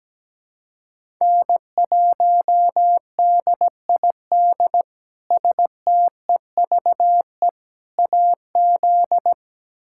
Morse code, heard as a keyed tone, about 17 wpm, 700 Hz. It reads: N1DID STEVE AZ
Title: K1USN SST QSO Practice - 17wpm